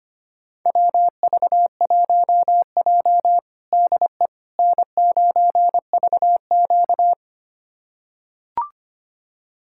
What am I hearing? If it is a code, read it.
Morse code, heard as a keyed tone, about 25 wpm, 700 Hz. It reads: WV1J DE N9VQ
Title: USA Callsign DE Callsign - 4 Characters - 25wpm